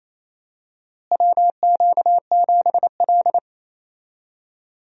Morse code, heard as a keyed tone, about 28 wpm, 700 Hz. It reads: WQ7L
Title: USA Call Signs - 4 Characters - Repeated 2x - 28wpm